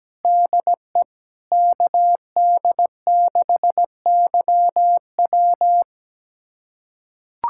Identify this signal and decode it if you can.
Morse code, heard as a keyed tone, about 17 wpm, 700 Hz. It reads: DE KD6YW
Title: USA Callsign DE Callsign - 5 Characters - 17wpm